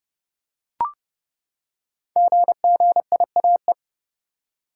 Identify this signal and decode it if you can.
Morse code, heard as a keyed tone, about 29 wpm, 700 Hz. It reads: GGIAE
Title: Code Groups - 5 Letters - 30wpm